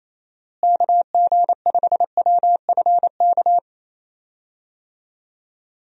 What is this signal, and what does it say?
Morse code, heard as a keyed tone, about 28 wpm, 700 Hz. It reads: KG5WFK
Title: USA Call Signs - 6 Characters - Repeated 2x - 28wpm